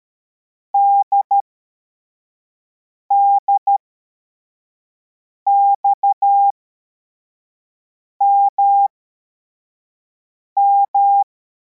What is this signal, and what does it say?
Morse code, heard as a keyed tone, about 13 wpm, spaced as for 4 wpm, 800 Hz. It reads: DDXMM
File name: formation / cw / Text25